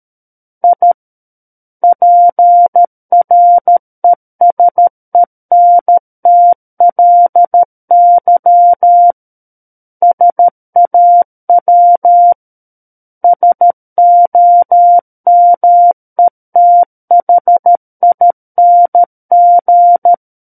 Morse code, heard as a keyed tone, about 13 wpm, 700 Hz. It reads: I PRESENTLY SAW SOMETHING
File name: Book One / 04-Chapter 4 - 13 WPM